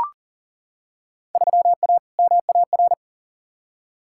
Morse code, heard as a keyed tone, about 40 wpm, 700 Hz. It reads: 3A MAR
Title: Field Day Exchanges Encore - 40wpm